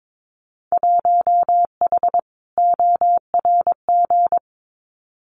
Morse code, heard as a keyed tone, about 22 wpm, 700 Hz. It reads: 1H ORG
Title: Winter Field Day Exchanges 22wpm